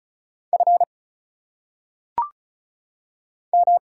Morse code, heard as a keyed tone, about 34 wpm, 700 Hz. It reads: F M
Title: Single Letters - Phonetic Pronunciation 35wpm